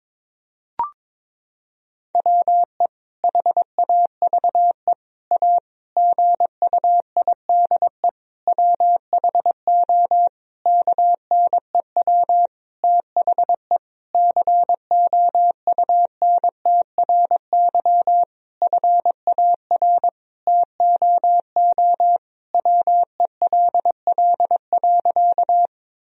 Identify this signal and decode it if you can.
Morse code, heard as a keyed tone, about 22 wpm, 700 Hz. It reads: WE HAVE A GUIDE WHO KNEW THE COUNTRY FAR TOO WELL.
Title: Sentences from Top 1000 Words - Review Part 3 - 22wpm